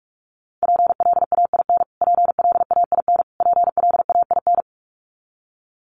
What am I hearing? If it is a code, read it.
Morse code, heard as a keyed tone, about 44 wpm, 700 Hz. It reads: PLAIN PLAIN PLAIN
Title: Top 500 Words Repeated 45wpm